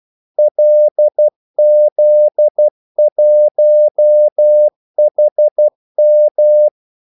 Morse code, heard as a keyed tone, about 12 wpm, 600 Hz. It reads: LZ1HM
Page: 2 ekstra ordmellemrum mellem kaldesignalerne